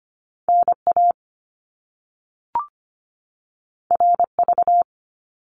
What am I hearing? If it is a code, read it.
Morse code, heard as a keyed tone, about 25 wpm, 700 Hz. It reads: NA RV
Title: Top Two Letter Combinations in Words 25wpm